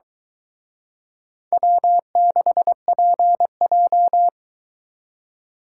Morse code, heard as a keyed tone, about 23 wpm, 700 Hz. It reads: W6PJ